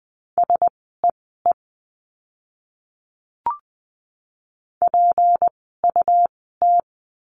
Morse code, heard as a keyed tone, about 20 wpm, spaced as for 15 wpm, 700 Hz. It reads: SEE PUT
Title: 237 Lesson 38f - Letter q - Words 20wpm